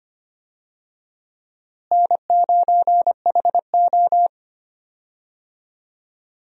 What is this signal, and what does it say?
Morse code, heard as a keyed tone, about 25 wpm, 700 Hz. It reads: N9HO
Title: USA Call Signs - 4 Characters - Repeated 2x - 25wpm